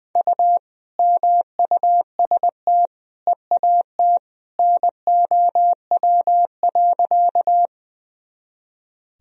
Morse code, heard as a keyed tone, about 21 wpm, 700 Hz. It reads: U MUST EAT NOW.
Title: Sentences from Top 300 Words and Abbreviations 20wpm